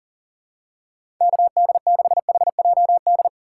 Morse code, heard as a keyed tone, about 39 wpm, 690 Hz. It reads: KD6HJD
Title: USA Call Signs - 6 Characters - 2x Character Spacing - 40wpm